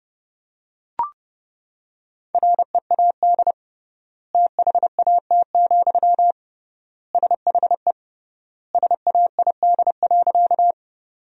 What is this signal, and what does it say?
Morse code, heard as a keyed tone, about 30 wpm, 700 Hz. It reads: READ THAT, SHE SAID.
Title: 3x Word-Spacing - Sentences from Top 200 Words - 30wpm